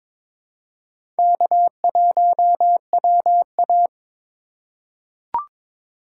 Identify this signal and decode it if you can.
Morse code, heard as a keyed tone, about 22 wpm, 700 Hz. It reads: K1WA
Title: USA Call Signs - 4 Characters - 2x Character Spacing - 22wpm